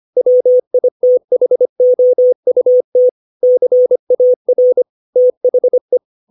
Morse code, heard as a keyed tone, about 25 wpm, 500 Hz. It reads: WITHOUT CAR THE
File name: 632 words / 632 Words 25WPM